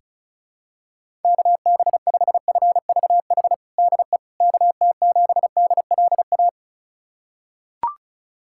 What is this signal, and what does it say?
Morse code, heard as a keyed tone, about 34 wpm, 700 Hz. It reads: KB5FVH DE KT7DLA
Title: USA Callsign DE Callsign 35wpm